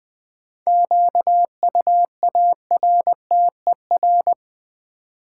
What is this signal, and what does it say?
Morse code, heard as a keyed tone, about 20 wpm, 700 Hz. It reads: QUARTER
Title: Sentences from Top 1300 Words - New Word Drill - 20wpm